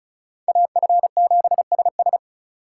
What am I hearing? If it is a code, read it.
Morse code, heard as a keyed tone, about 44 wpm, 700 Hz. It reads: AF7SS
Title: Speed Racing - USA 2x2 Call Signs 35wpm